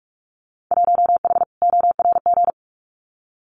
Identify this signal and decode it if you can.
Morse code, heard as a keyed tone, about 44 wpm, 700 Hz. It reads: 1H ORG
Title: Winter Field Day Exchanges 45wpm